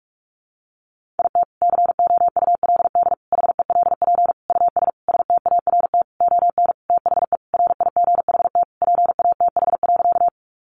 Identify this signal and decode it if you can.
Morse code, heard as a keyed tone, about 44 wpm, 700 Hz. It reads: IT COULD HELP US START ON THE RIGHT PATH.
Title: Sentences from Top 1400 Words 45wpm